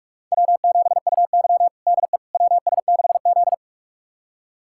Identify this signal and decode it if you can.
Morse code, heard as a keyed tone, about 41 wpm, 700 Hz. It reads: W7UY DE WS6Z
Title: USA Callsign DE Callsign - 4 Characters - 45wpm